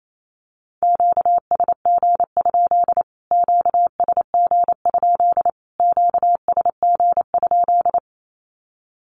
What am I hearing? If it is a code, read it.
Morse code, heard as a keyed tone, about 28 wpm, 700 Hz. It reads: QSG? QSG? QSG?
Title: Q-Codes Repeated 3x 28wpm